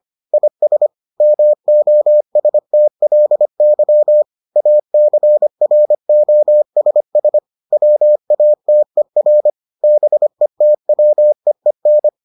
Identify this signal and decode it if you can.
Morse code, heard as a keyed tone, about 25 wpm, 600 Hz. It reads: IS MOSTLY ACROSS WATER BETWEEN